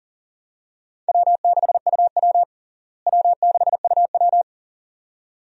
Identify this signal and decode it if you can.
Morse code, heard as a keyed tone, about 40 wpm, 700 Hz. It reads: W6UW W6UW
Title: USA Call Signs - 4 Characters - Repeated 2x - 40wpm